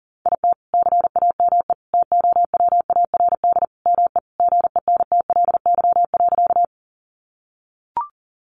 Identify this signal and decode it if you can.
Morse code, heard as a keyed tone, about 40 wpm, 700 Hz. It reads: IT CAME TOWARD ME GENTLY.
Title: Sentences from Top 1500 Words 40wpm